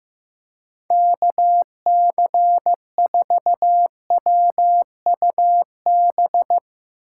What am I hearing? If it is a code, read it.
Morse code, heard as a keyed tone, about 16 wpm, 700 Hz. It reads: KC4WUB
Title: Speed Racing - USA 2x3 Call Signs 15wpm